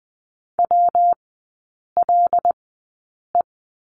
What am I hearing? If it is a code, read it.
Morse code, heard as a keyed tone, about 20 wpm, spaced as for 8 wpm, 690 Hz. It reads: WLE
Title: 166 Lesson 27e - Number 7 - Three Characters 20wpm